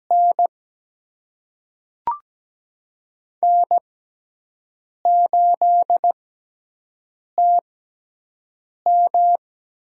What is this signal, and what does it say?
Morse code, heard as a keyed tone, about 17 wpm, spaced as for 6 wpm, 700 Hz. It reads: N N8TM
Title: USA Call Signs - 6x Character Spacing - 17wpm